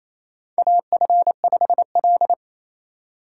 Morse code, heard as a keyed tone, about 28 wpm, 700 Hz. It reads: AF5L
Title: USA Call Signs - 6x Character Spacing - 28wpm